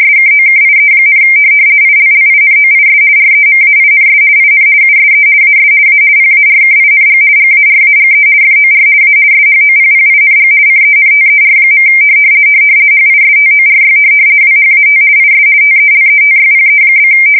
RTTY, about 50 baud. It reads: SKOKIE, ILLINOIS REMAINS IN USE AS A PARKING GARAGE FOR A SHOPPING CENTER. HISTORY WHAT IS TELETYPE?
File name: fsk170-50B / teletype-corp